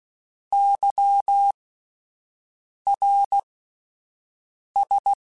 Morse code, heard as a keyed tone, about 16 wpm, spaced as for 5 wpm, 800 Hz. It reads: YRS